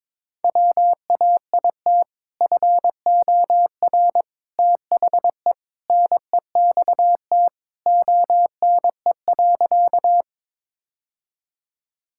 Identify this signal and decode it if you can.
Morse code, heard as a keyed tone, about 22 wpm, 700 Hz. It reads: WAIT FOR THE NEXT ONE.
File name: practice / 22wpm / Sentences with 2-Word Phrases - 22wpm